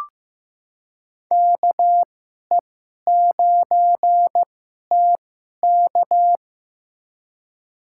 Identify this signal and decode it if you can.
Morse code, heard as a keyed tone, about 15 wpm, spaced as for 11 wpm, 700 Hz. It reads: KE9TK
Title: USA Call Signs - 2x Character Spacing - 15wpm